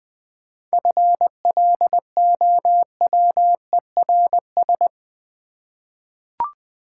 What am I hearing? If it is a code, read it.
Morse code, heard as a keyed tone, about 20 wpm, 700 Hz. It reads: FLOWERS
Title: Easy Words with Hints - 20wpm